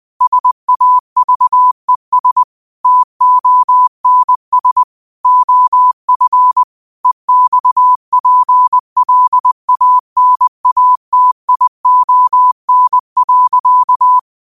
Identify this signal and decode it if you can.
Morse code, heard as a keyed tone, about 20 wpm, 1000 Hz. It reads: SAVES TONS OF EXPLANATION.